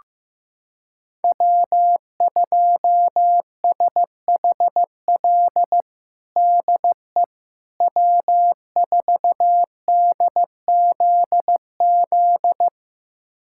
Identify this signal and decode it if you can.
Morse code, heard as a keyed tone, about 15 wpm, 700 Hz. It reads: W2SHL DE W4DZZ
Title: USA Callsign DE Callsign - 5 Characters - 15wpm